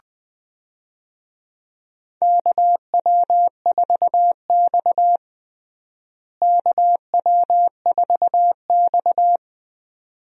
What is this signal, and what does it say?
Morse code, heard as a keyed tone, about 20 wpm, 700 Hz. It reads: KW4X KW4X
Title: USA Call Signs - 4 Characters - Repeated 2x - 20wpm